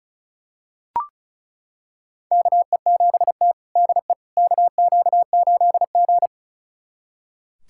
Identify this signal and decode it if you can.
Morse code, heard as a keyed tone, about 36 wpm, 700 Hz. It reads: KE7T DE KQ8G
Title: USA Callsign DE Callsign - 4 Characters - 35wpm